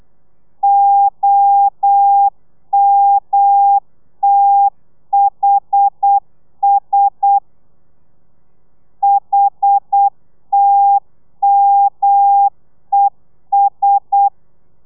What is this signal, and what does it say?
Morse code, heard as a keyed tone, about 8 wpm, 800 Hz. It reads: OMTHS HTMES
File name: telegrafia / i2tfj / Lez1 35